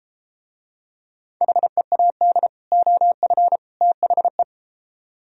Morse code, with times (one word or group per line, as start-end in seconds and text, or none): 1.4-2.5 HEAD
2.7-3.6 OF
3.8-4.4 THE